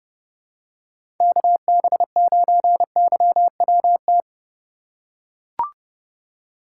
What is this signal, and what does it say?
Morse code, heard as a keyed tone, about 30 wpm, 700 Hz. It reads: KB9YWT
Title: USA Call Signs - 4x Character Spacing - 30wpm